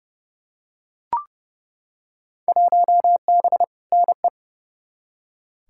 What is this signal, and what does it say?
Morse code, heard as a keyed tone, about 31 wpm, 700 Hz. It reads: 1B NE
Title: Field Day Exchanges Encore - 30wpm